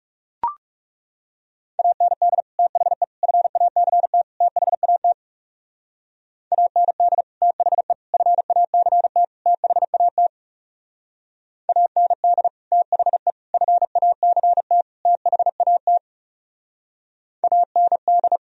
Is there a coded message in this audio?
Yes: Morse code from 1.8 to 18.4 s, heard as a keyed tone, about 38 wpm, 700 Hz. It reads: AND THE FACT THAT AND THE FACT THAT AND THE FACT THAT AND